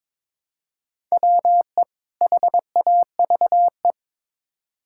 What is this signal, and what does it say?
Morse code, heard as a keyed tone, about 22 wpm, 700 Hz. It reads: WE HAVE
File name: Build A Phrase - 2-Word Phrases - 22wpm